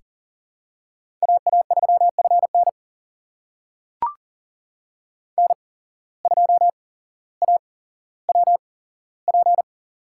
Morse code, heard as a keyed tone, about 40 wpm, spaced as for 17 wpm, 700 Hz. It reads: AA3FN N2AWP
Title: USA Call Signs - 5 Characters - 8x Character Spacing - 40wpm